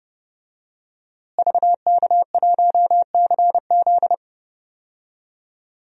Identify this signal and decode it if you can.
Morse code, heard as a keyed tone, about 30 wpm, 700 Hz. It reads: VK1CZ
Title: DXCC-100 Call Signs - 5 Characters - Repeated 3x - 30wpm